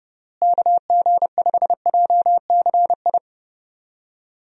Morse code, heard as a keyed tone, about 32 wpm, 700 Hz. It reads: KG5JCI
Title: USA Call Signs - 6 Characters - Repeated 3x - 30wpm